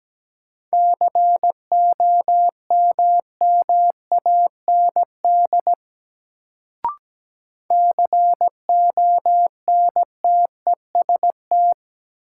Morse code, heard as a keyed tone, about 17 wpm, 700 Hz. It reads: COMMAND CONTEST
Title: Sentences from Top 1600 Words - New Word Drill Alt - 17wpm